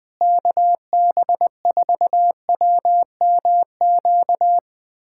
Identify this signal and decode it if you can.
Morse code, heard as a keyed tone, about 21 wpm, 700 Hz. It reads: KB4WMQ
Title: USA Callsign DE Callsign - 6 Characters - 20wpm